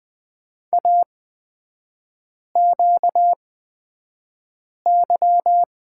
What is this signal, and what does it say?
Morse code, heard as a keyed tone, about 20 wpm, spaced as for 5 wpm, 700 Hz. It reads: AQY
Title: 236 Lesson 38e - Letter q - Three Characters 20wpm